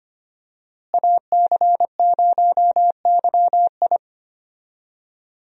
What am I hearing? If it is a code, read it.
Morse code, heard as a keyed tone, about 25 wpm, 700 Hz. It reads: AC0YI
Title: USA Call Signs - 5 Characters - Repeated 2x - 25wpm